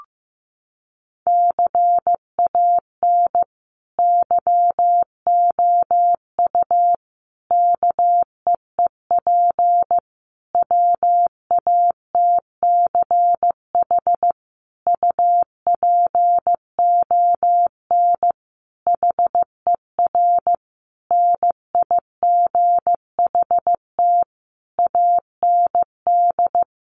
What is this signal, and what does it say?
Morse code, heard as a keyed tone, about 14 wpm, 700 Hz. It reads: CAN YOU KEEP WATCH UPON HER NIGHT AND
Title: Sentences from Top 1100 Words 15wpm